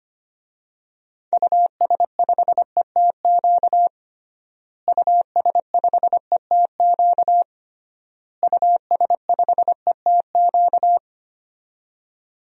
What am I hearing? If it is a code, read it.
Morse code, heard as a keyed tone, about 25 wpm, 700 Hz. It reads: US5ETQ US5ETQ US5ETQ
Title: DXCC-100 Call Signs - 6 Characters - Repeated 3x - 25wpm